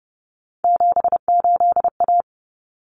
Morse code, heard as a keyed tone, about 30 wpm, 700 Hz. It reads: 78A
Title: Common Cut Numbers - 3-Digit 30wpm